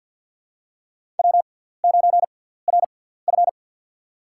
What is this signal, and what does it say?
Morse code, heard as a keyed tone, about 53 wpm, spaced as for 17 wpm, 700 Hz. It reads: W9RF
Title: USA Call Signs - 4 Characters - 6x Character Spacing - 50wpm